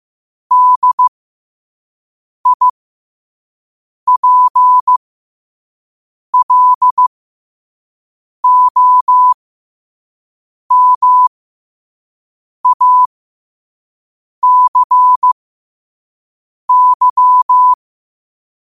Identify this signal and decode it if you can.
Morse code, heard as a keyed tone, about 15 wpm, spaced as for 5 wpm, 1000 Hz. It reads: DIPLOMACY
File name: qotdfiles / 5wpm